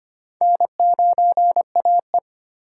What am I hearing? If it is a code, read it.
Morse code, heard as a keyed tone, about 25 wpm, 700 Hz. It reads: N9AE
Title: USA Call Signs - 4 Characters - 3x Character Spacing - 25wpm